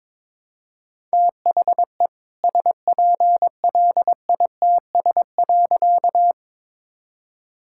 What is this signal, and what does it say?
Morse code, heard as a keyed tone, about 22 wpm, 700 Hz. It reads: THE SPLITS.
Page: morse-code-ninja-practice-22wpm